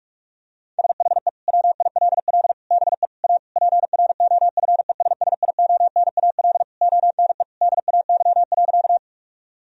Morse code, heard as a keyed tone, about 46 wpm, 700 Hz. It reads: SHE WILL BE A PROFESSIONAL ONE DAY.
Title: Sentences from Top 1000 Words - Review Part 4 - 45wpm